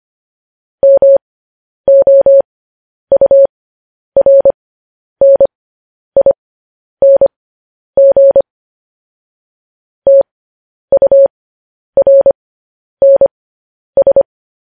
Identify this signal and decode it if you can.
Morse code, heard as a keyed tone, about 25 wpm, spaced as for 10 wpm, 550 Hz. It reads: MOURNING TURNS
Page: news headlines in morse code at 10 wpm